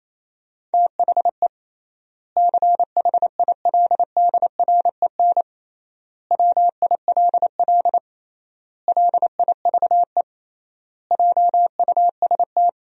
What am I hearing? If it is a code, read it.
Morse code, heard as a keyed tone, about 28 wpm, 700 Hz. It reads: THE CHILDREN WILL LIVE JUST
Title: 3x Word-Spacing - Sentences from Top 300 Words - 28wpm